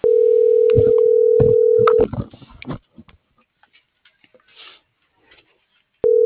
Ringback tone (North America), 440 + 480 Hz.